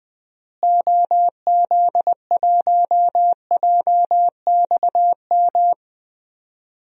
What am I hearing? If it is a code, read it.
Morse code, heard as a keyed tone, about 21 wpm, 700 Hz. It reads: OZ1JXM
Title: DXCC-100 Call Signs - 6 Characters - Repeated 2x - 20wpm